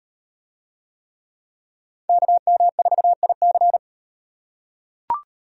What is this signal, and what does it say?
Morse code, heard as a keyed tone, about 33 wpm, 700 Hz. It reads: KM4IC